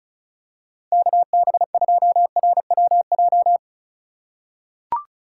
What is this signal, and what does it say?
Morse code, heard as a keyed tone, about 34 wpm, 700 Hz. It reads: KB2RWJ
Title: USA Call Signs - 6 Characters - 2x Character Spacing - 35wpm